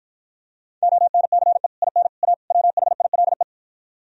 Morse code, heard as a keyed tone, about 52 wpm, 700 Hz. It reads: ONCE IN A WHILE